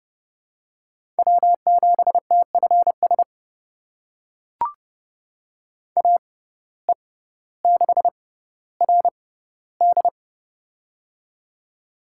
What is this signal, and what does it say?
Morse code, heard as a keyed tone, about 30 wpm, spaced as for 15 wpm, 700 Hz. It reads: W7TFS AE6RD
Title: USA Call Signs - 5 Characters - 6x Character Spacing - 30wpm